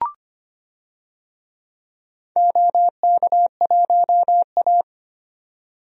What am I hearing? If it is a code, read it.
Morse code, heard as a keyed tone, about 25 wpm, 700 Hz. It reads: OK1A
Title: DXCC-100 Call Signs - 4 Characters - 25wpm